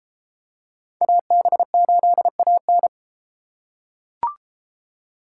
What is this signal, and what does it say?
Morse code, heard as a keyed tone, about 33 wpm, 700 Hz. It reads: AB8AN